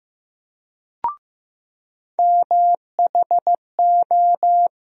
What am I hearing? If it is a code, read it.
Morse code, heard as a keyed tone, about 15 wpm, 700 Hz. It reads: MHO
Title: USA Call Sign Postfixes - 3 Letters 15wpm